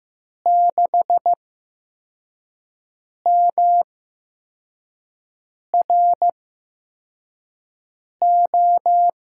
Morse code, heard as a keyed tone, about 15 wpm, spaced as for 4 wpm, 700 Hz. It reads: 6MRO
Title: USA Call Signs - 5 Characters - 8x Character Spacing - 15wpm